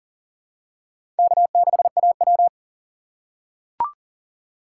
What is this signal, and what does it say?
Morse code, heard as a keyed tone, about 40 wpm, 700 Hz. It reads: K6AW
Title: USA Call Signs - 4 Characters - 6x Character Spacing - 40wpm